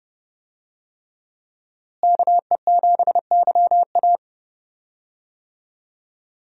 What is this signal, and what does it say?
Morse code, heard as a keyed tone, about 30 wpm, 700 Hz. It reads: KE7YA
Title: USA Call Signs - 5 Characters - Repeated 3x - 30wpm